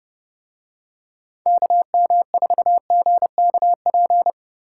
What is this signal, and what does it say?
Morse code, heard as a keyed tone, about 29 wpm, 700 Hz. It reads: KM4GKP